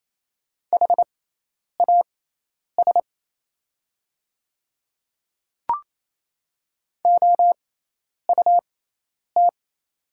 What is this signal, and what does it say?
Morse code, heard as a keyed tone, about 28 wpm, spaced as for 10 wpm, 700 Hz. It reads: HAS OUT
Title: 6x Character Spacing - Top 100 Words - 28wpm